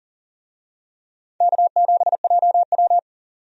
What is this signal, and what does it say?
Morse code, heard as a keyed tone, about 40 wpm, 700 Hz. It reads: K7JW
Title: USA Call Signs - 4 Characters - 40wpm